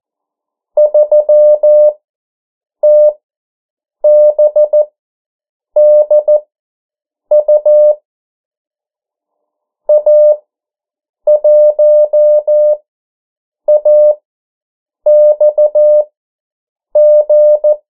Morse code, heard as a keyed tone, about 14 wpm, spaced as for 7 wpm, 600 Hz. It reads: 3TBDU A1AXG
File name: morse mp3 / 09 urok 8